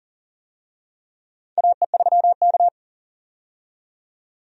Morse code, heard as a keyed tone, about 40 wpm, 700 Hz. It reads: AE3K
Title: USA Call Signs - 4 Characters - 40wpm